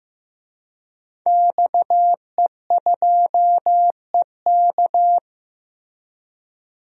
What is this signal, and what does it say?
Morse code, heard as a keyed tone, about 15 wpm, 700 Hz. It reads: XE2EK
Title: DXCC-100 Call Signs - 5 Characters - Repeated 3x - 15wpm